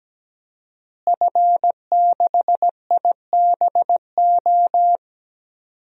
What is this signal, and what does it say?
Morse code, heard as a keyed tone, about 17 wpm, 700 Hz. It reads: F6IBO
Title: DXCC-100 Call Signs - 5 Characters - Repeated 2x - 17wpm